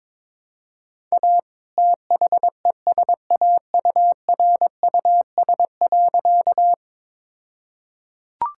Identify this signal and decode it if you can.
Morse code, heard as a keyed tone, about 21 wpm, 700 Hz. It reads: A THESAURUS.
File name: All-Age Appropriate Jokes - 22wpm